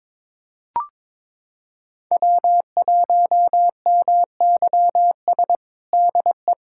Morse code, heard as a keyed tone, about 22 wpm, 700 Hz. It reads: W1MYS DE
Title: USA Callsign DE Callsign - 5 Characters - 22wpm